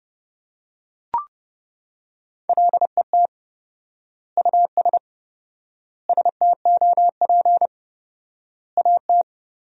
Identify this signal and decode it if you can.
Morse code, heard as a keyed tone, about 30 wpm, 700 Hz. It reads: LET US STOP AT
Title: 4x Word-Spacing - Sentences from Top 400 Words - 30wpm